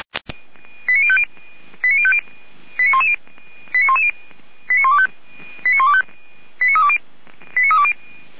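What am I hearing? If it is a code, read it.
Five-tone selective call (ZVEI), heard as quick runs of five tones, tones of about 70 ms. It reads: 89050 89050 89110 89110 89125 89320 89320
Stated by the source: ZVEI five-tone selective call received by SDR, from the Feuerwehr Orlamünde (fire brigade) audio server.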